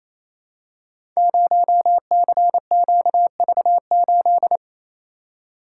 Morse code, heard as a keyed tone, about 29 wpm, 700 Hz. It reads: 0CQV8
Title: Code Groups - 5 Letter:Numbers - 28wpm